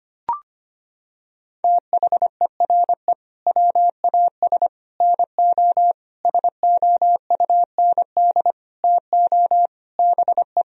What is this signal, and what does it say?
Morse code, heard as a keyed tone, about 25 wpm, 700 Hz. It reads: THERE WAS NO SOUND TO BE
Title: Sentences from Top 1000 Words - Review Part 3 - 25wpm